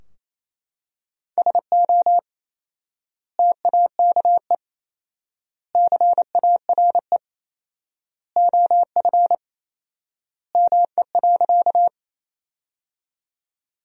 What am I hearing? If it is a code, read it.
Morse code, heard as a keyed tone, about 28 wpm, 700 Hz. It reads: SO TAKE CARE OF ME.
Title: 4x Word-Spacing - Sentences from Top 300 Words - 28wpm